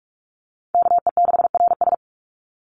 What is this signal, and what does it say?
Morse code, heard as a keyed tone, about 44 wpm, 700 Hz. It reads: KE6RS
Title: NA SOTA Chaser Callsigns 45wpm